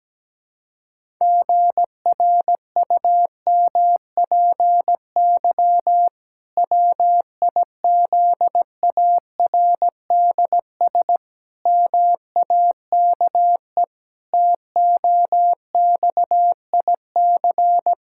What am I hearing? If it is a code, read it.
Morse code, heard as a keyed tone, about 17 wpm, 700 Hz. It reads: GRUMPY WIZARDS MAKE TOXIC
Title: Pangram Challenge v2 - 17wpm